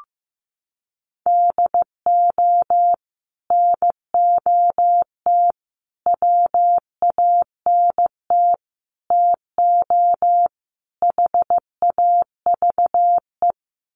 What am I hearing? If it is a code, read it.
Morse code, heard as a keyed tone, about 15 wpm, 700 Hz. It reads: DO NOT WANT TO HAVE
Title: Sets of 5 Words 15wpm